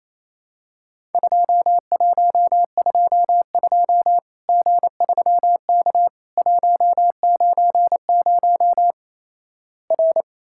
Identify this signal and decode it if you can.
Morse code, heard as a keyed tone, about 28 wpm, 700 Hz. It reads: 2122 G3K 190 R
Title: Worked All Europe DX Contest - QTC Exchange - 28wpm